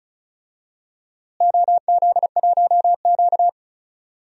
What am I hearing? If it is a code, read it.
Morse code, heard as a keyed tone, about 36 wpm, 700 Hz. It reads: OZ1Q
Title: DXCC-100 Call Signs - 4 Characters - Repeated 3x - 35wpm